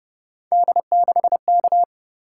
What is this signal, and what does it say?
Morse code, heard as a keyed tone, about 31 wpm, 700 Hz. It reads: D6K
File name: Speed-Racing - Three Letter-Number Combinations - 25wpm